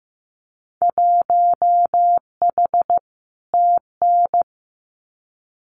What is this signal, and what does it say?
Morse code, heard as a keyed tone, about 15 wpm, 700 Hz. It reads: 1H TN
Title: Winter Field Day Exchanges 15wpm